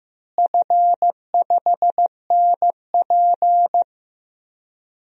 Morse code, heard as a keyed tone, about 15 wpm, 700 Hz. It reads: F5NP
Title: DXCC-100 Call Signs - 4 Characters - Repeated 3x - 15wpm